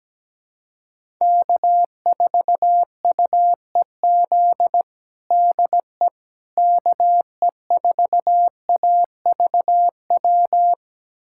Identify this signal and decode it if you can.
Morse code, heard as a keyed tone, about 17 wpm, 700 Hz. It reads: K4UEZ DE KE4AVW
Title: USA Callsign DE Callsign 17wpm